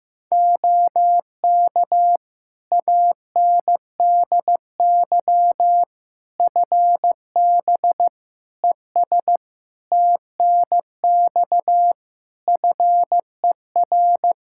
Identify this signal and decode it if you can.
Morse code, heard as a keyed tone, about 15 wpm, 700 Hz. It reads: OK ANDY FB ES TNX FER
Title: Practice QSO - Exchange 2 - Info - 15wpm